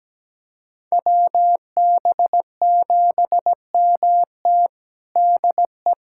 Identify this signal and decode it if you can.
Morse code, heard as a keyed tone, about 17 wpm, 700 Hz. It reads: WB7MT DE
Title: USA Callsign DE Callsign 17wpm